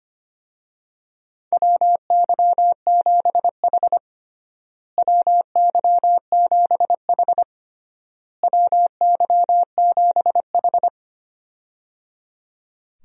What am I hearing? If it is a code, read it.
Morse code, heard as a keyed tone, about 25 wpm, 700 Hz. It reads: WY7H WY7H WY7H
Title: USA Call Signs - 4 Characters - Repeated 3x - 25wpm